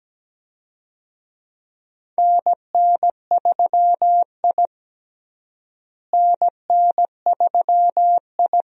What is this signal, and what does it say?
Morse code, heard as a keyed tone, about 17 wpm, 700 Hz. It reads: NN3I NN3I
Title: USA Call Signs - 4 Characters - Repeated 3x - 17wpm